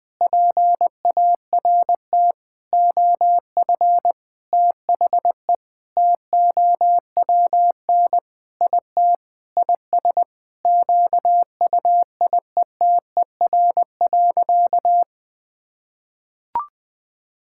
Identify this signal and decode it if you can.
Morse code, heard as a keyed tone, about 21 wpm, 700 Hz. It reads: PART OF THE TOWN IT IS QUIETER.
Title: Sentences with 5-Word Phrases - Spoken Beginning - 20wpm